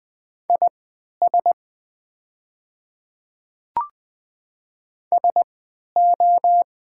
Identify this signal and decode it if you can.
Morse code, heard as a keyed tone, about 20 wpm, spaced as for 11 wpm, 690 Hz. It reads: IS SO
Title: 3x Character Spacing - Top 300 Words - 20wpm